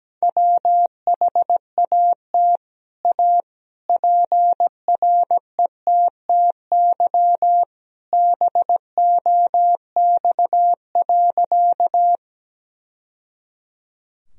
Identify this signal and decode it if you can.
Morse code, heard as a keyed tone, about 17 wpm, 700 Hz. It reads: WHAT A PRETTY BOX.
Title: Sentences from Top 1000 Words - Review Part 2 - 17wpm